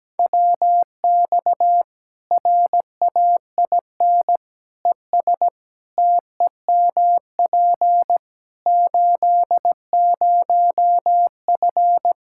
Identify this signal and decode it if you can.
Morse code, heard as a keyed tone, about 17 wpm, 700 Hz. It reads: WX RAIN ES TEMP 80F
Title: Practice QSO - Exchange 2 - Info - 17wpm